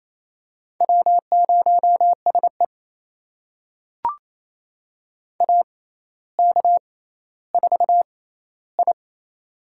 Morse code, heard as a keyed tone, about 27 wpm, spaced as for 14 wpm, 700 Hz. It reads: W0SE AK4I
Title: USA Call Signs - 4 Characters - 6x Character Spacing - 28wpm